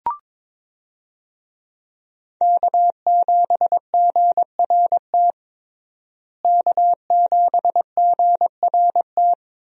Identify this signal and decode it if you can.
Morse code, heard as a keyed tone, about 21 wpm, 690 Hz. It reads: K7GRT K7GRT
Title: USA Call Signs - 5 Characters - Repeated 3x - 22wpm